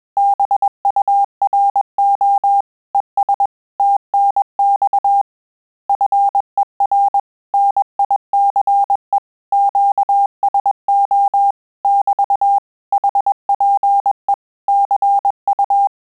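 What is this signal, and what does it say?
Morse code, heard as a keyed tone, about 21 wpm, 800 Hz. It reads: BURO ES TNX FER NICE QSO <BT> HPE CU
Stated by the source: Morse QSO practice at 20 words per minute, in Italian and English.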